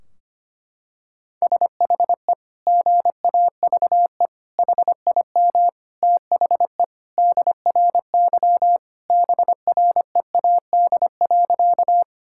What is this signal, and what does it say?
Morse code, heard as a keyed tone, about 25 wpm, 700 Hz. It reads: SHE GAVE HIM THE DRY BREAD.
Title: Sentences from Top 1600 Words 25wpm